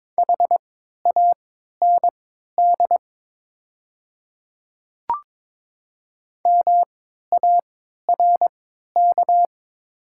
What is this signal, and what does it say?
Morse code, heard as a keyed tone, about 22 wpm, spaced as for 13 wpm, 700 Hz. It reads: HAND MARK
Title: 3x Character Spacing - Top 300 Words - 22wpm